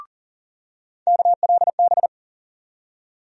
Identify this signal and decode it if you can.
Morse code, heard as a keyed tone, about 40 wpm, 700 Hz. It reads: KLB
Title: USA Call Sign Postfixes - 3 Letters 40wpm